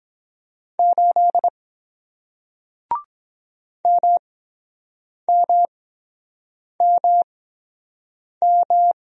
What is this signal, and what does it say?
Morse code, heard as a keyed tone, about 23 wpm, spaced as for 7 wpm, 700 Hz. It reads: 8 MMMM